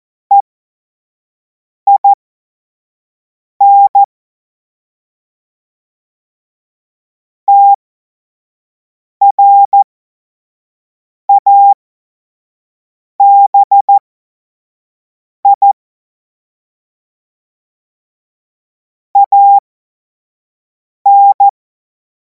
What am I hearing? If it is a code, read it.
Morse code, heard as a keyed tone, about 14 wpm, spaced as for 5 wpm, 800 Hz. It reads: EIN TRABI AN
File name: Morsekurs / Witze / Witz 07. Juli 2026 25 F70